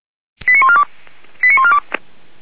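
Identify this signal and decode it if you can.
Five-tone selective call (ZVEI), heard as quick runs of five tones, tones of about 70 ms. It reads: 89142 89142
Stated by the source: ZVEI five-tone selective call received on SDR, channel 464, code 85753.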